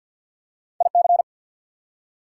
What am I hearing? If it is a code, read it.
Morse code, heard as a keyed tone, about 53 wpm, 690 Hz. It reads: IC
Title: Prefixes and Suffixes 50wpm